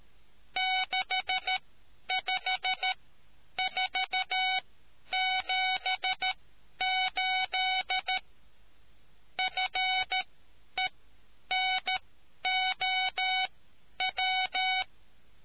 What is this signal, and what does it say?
Morse code, heard as a keyed tone, about 13 wpm, spaced as for 9 wpm, 750 Hz. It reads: 65478 FENOW